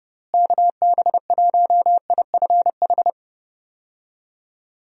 Morse code, heard as a keyed tone, about 30 wpm, 700 Hz. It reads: KB1IFH
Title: USA Call Signs - 6 Characters - Repeated 3x - 30wpm